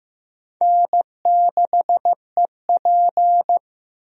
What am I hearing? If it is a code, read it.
Morse code, heard as a keyed tone, about 15 wpm, 700 Hz. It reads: N6EP
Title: USA Call Signs - 4 Characters - 15wpm